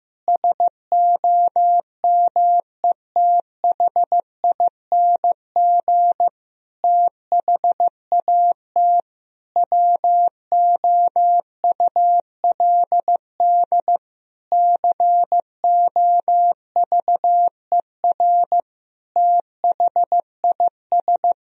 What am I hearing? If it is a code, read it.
Morse code, heard as a keyed tone, about 15 wpm, 700 Hz. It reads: SOMETHING THAT WOULD COVER THIS
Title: Sentences from Top 1000 Words - Review Part 1 - 15wpm